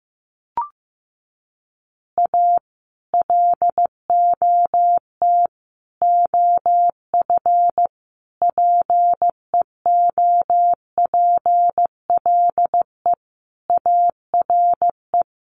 Morse code, heard as a keyed tone, about 15 wpm, 700 Hz. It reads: A LOT OF PEOPLE ARE
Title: Additional Easy Phrases 15wpm